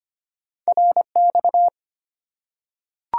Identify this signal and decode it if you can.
Morse code, heard as a keyed tone, about 25 wpm, 700 Hz. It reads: RX
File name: Top 90 QSO Elements - 3x Character Spacing - 25wpm